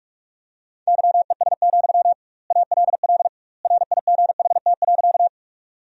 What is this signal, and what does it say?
Morse code, heard as a keyed tone, about 46 wpm, 700 Hz. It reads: YES, ALL RIGHT.
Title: 2x Word-Spacing - Sentences from Top 500 Words - 45wpm